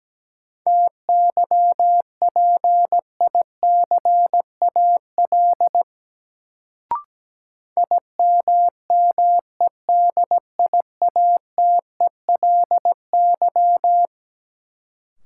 Morse code, heard as a keyed tone, about 17 wpm, 700 Hz. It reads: TYPICAL IMMEDIATELY
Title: Sentences from Top 1100 Words - New Word Drill - 17wpm